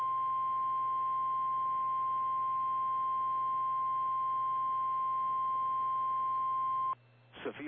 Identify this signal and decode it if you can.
Emergency Alert System (SAME) tones (the 1050 Hz attention tone). It is heard from 0.3 to 6.8 s.
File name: eas / KKPRKXPN / KKPRKXPN